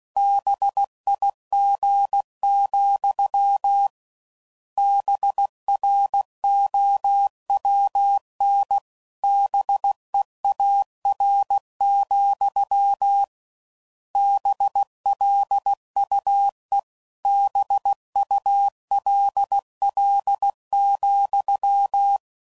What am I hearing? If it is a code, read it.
Morse code, heard as a keyed tone, about 16 wpm, 800 Hz. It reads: BIG, BROWN BEAR, BLUE BULL,